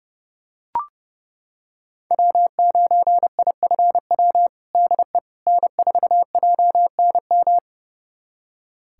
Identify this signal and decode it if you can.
Morse code, heard as a keyed tone, about 32 wpm, 700 Hz. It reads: W9IFW DE N4JNM
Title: USA Callsign DE Callsign - 5 Characters - 30wpm